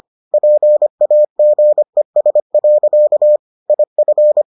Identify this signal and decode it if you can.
Morse code, heard as a keyed tone, about 25 wpm, 600 Hz. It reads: PAGES. IF